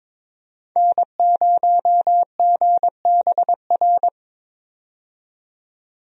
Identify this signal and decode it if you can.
Morse code, heard as a keyed tone, about 23 wpm, 700 Hz. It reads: N0GBR
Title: USA Call Signs - 5 Characters - 22wpm